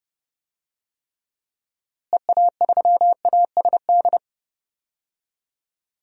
Morse code, heard as a keyed tone, about 30 wpm, 700 Hz. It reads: EA3ASD
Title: DXCC-100 Call Signs - 6 Characters - Repeated 3x - 30wpm